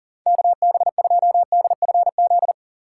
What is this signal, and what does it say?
Morse code, heard as a keyed tone, about 40 wpm, 700 Hz. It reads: KB2DFZ
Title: USA Call Signs - 6 Characters - 2x Character Spacing - 40wpm